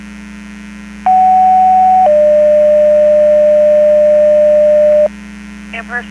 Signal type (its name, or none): two-tone page